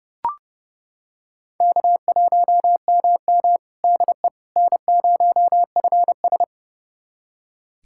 Morse code, heard as a keyed tone, about 32 wpm, 700 Hz. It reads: K1MM DE N0FS
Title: USA Callsign DE Callsign - 4 Characters - 30wpm